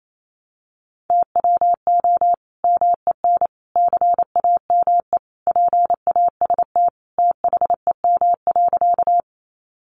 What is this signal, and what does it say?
Morse code, heard as a keyed tone, about 28 wpm, 700 Hz. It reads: TWO MEN CAME PAST THEM.
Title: Sentences from Top 600 Words 28wpm